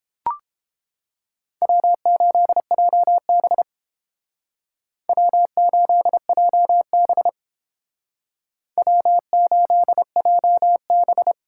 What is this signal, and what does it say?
Morse code, heard as a keyed tone, about 28 wpm, 700 Hz. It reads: W8JB W8JB W8JB